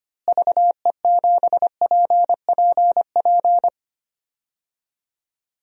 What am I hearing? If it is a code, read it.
Morse code, heard as a keyed tone, about 25 wpm, 700 Hz. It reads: VE7PPP
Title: DXCC-100 Call Signs - 6 Characters - Repeated 3x - 25wpm